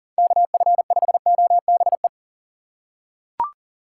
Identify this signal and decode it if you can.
Morse code, heard as a keyed tone, about 37 wpm, 700 Hz. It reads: KF5OBE
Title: USA Call Signs - 6 Characters - 3x Character Spacing - 40wpm